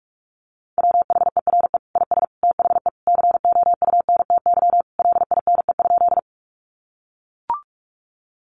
Morse code, heard as a keyed tone, about 44 wpm, 700 Hz. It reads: WHERE IS THE COUNTY LINE?
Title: Sentences from Top 1200 Words 45wpm